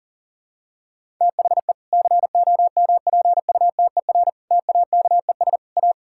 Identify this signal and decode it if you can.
Morse code, heard as a keyed tone, about 40 wpm, 700 Hz. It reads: THE COMPUTER TAKES A
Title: Sentences from Top 1000 Words - Review Part 6 - 40wpm